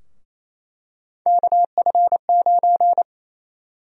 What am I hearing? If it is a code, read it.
Morse code, heard as a keyed tone, about 28 wpm, 700 Hz. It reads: KF9
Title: USA Call Sign Prefixes - Repeated 2x - 28wpm